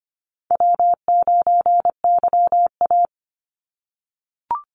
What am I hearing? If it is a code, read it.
Morse code, heard as a keyed tone, about 25 wpm, 700 Hz. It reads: W9YA
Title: Easy Call Signs - 25wpm